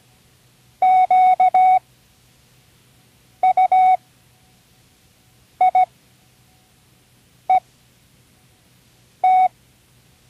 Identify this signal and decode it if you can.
Morse code, heard as a keyed tone, about 17 wpm, spaced as for 5 wpm, 740 Hz. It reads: QUIET
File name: CWResources / Learning Morse Code (from Tune in the World) / 29 words using all letters